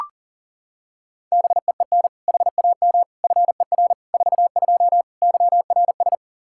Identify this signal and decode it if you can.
Morse code, heard as a keyed tone, about 40 wpm, 700 Hz. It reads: BEEN HAM FER 42 YRS
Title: Practice QSO - Exchange 3 - More Info - 40wpm